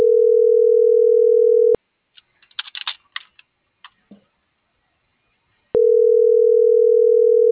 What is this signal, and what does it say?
Ringback tone (North America), 440 + 480 Hz.